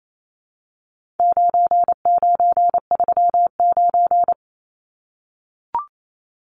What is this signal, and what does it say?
Morse code, heard as a keyed tone, about 28 wpm, 700 Hz. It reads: 9939
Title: Numbers - 4-Digit 28wpm